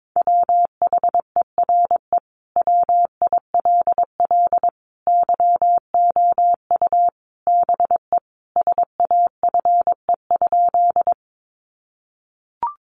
Morse code, heard as a keyed tone, about 22 wpm, 700 Hz. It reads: WHERE WILL YOU BE SAFE?
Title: Sentences from Top 900 Words 22wpm